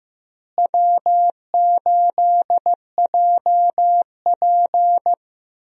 Morse code, heard as a keyed tone, about 15 wpm, 700 Hz. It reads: W8JP
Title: USA Call Signs - 4 Characters - Repeated 3x - 15wpm